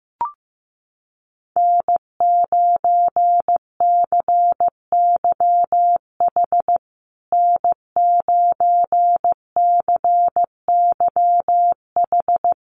Morse code, heard as a keyed tone, about 15 wpm, 700 Hz. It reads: N9CYH N9CYH
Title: USA Call Signs Repeated 3x 15wpm